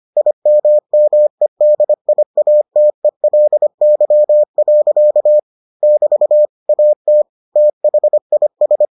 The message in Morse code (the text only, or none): IMMEDIATELY. <BT> AT THIS